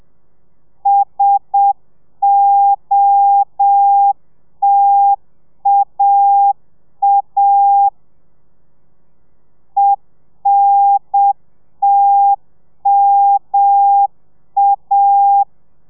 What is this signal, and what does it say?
Morse code, heard as a keyed tone, about 7 wpm, 800 Hz. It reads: SOTAA ENTMA